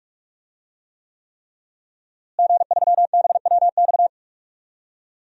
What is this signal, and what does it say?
Morse code, heard as a keyed tone, about 43 wpm, 700 Hz. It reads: G3BWX
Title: DXCC-100 Call Signs - 5 Characters - Repeated 3x - 45wpm